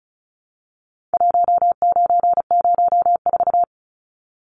Morse code, heard as a keyed tone, about 34 wpm, 700 Hz. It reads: 1904
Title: Numbers - 4-Digit 35wpm